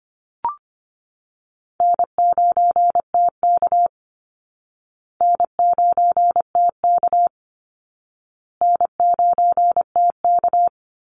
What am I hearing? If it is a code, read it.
Morse code, heard as a keyed tone, about 25 wpm, 700 Hz. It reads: N9TK N9TK N9TK
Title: Easy Call Signs Repeated 3x - 25wpm